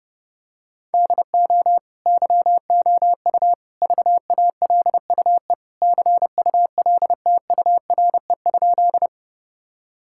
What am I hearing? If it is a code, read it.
Morse code, heard as a keyed tone, about 32 wpm, 700 Hz. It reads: DO YOU VALUE CULTURE?
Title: Sentences from Top 1000 Words - Review Part 6 - 30wpm